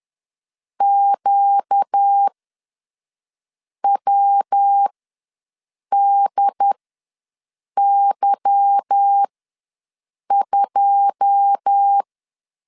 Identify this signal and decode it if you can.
Morse code, heard as a keyed tone, about 10 wpm, spaced as for 5 wpm, 800 Hz. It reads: QWDY2
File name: LEZIONI / 9 2 ws 10wpm 4130